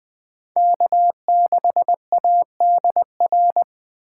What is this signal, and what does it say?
Morse code, heard as a keyed tone, about 20 wpm, 700 Hz. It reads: K6ADR